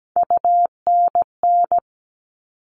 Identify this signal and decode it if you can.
Morse code, heard as a keyed tone, about 17 wpm, 700 Hz. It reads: UNN
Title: All Cut Numbers - 3-Digit 17wpm